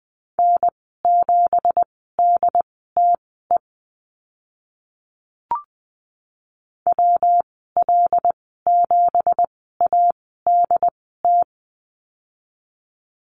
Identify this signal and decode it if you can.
Morse code, heard as a keyed tone, about 20 wpm, spaced as for 15 wpm, 700 Hz. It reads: N7DTE WL7ADT
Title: 173 Lesson 28g - Number 9 - Call Signs 20wpm